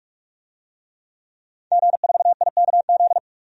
Morse code, heard as a keyed tone, about 46 wpm, 700 Hz. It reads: G4IKZ
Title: DXCC-100 Call Signs - 5 Characters - 45wpm